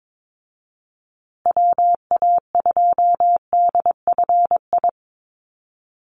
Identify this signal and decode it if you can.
Morse code, heard as a keyed tone, about 22 wpm, 700 Hz. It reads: WA2DFI
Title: Call Signs Repeated 2x - 22wpm